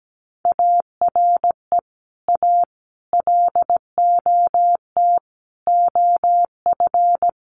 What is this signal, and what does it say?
Morse code, heard as a keyed tone, about 17 wpm, 700 Hz. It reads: ARE A LOT OF
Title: Sets of 5 Words Encore2 17wpm